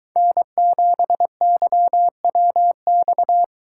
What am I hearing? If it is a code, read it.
Morse code, heard as a keyed tone, about 23 wpm, 700 Hz. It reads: N7YWX